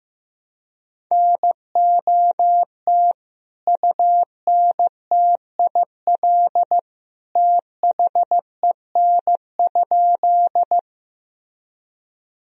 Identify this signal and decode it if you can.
Morse code, heard as a keyed tone, about 15 wpm, 700 Hz. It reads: NOT UNTIL THEN?
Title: Sentences from Top 1000 Words - Review Part 1 - 15wpm